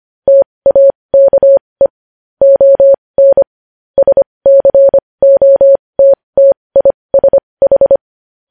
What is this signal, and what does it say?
Morse code, heard as a keyed tone, about 25 wpm, spaced as for 20 wpm, 550 Hz. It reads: TAKE ON SCOTTISH
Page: News Headlines in Morse Code at 20 WPM